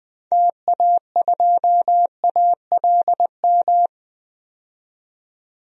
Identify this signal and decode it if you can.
Morse code, heard as a keyed tone, about 20 wpm, 700 Hz. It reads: TA2ALM
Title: DXCC-100 Call Signs - 6 Characters - 20wpm